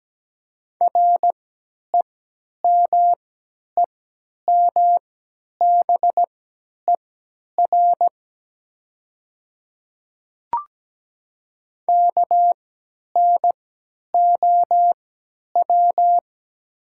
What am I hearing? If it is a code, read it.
Morse code, heard as a keyed tone, about 17 wpm, spaced as for 10 wpm, 700 Hz. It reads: REMEMBER KNOW
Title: 3x Character Spacing - Top 500 Words - 17wpm